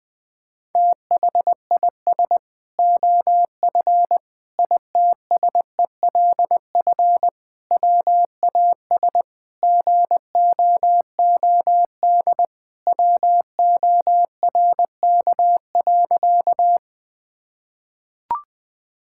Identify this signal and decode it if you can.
Morse code, heard as a keyed tone, about 20 wpm, 700 Hz. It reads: THIS OF ITSELF WAS GOOD WORK.
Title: Sentences from Top 1000 Words - Review Part 5 - 20wpm